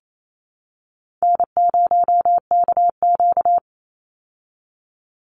Morse code, heard as a keyed tone, about 28 wpm, 700 Hz. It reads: N0KQ
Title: Easy Call Signs Repeated 3x - 28wpm